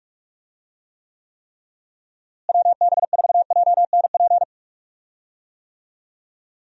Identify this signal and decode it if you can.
Morse code, heard as a keyed tone, about 43 wpm, 700 Hz. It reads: WB4JNP
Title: USA Call Signs - 6 Characters - Repeated 3x - 45wpm